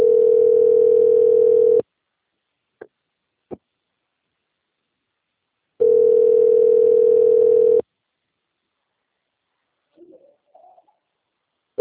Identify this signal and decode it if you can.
Ringback tone (North America), 440 + 480 Hz.
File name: rg-1600-unknown- - - .35861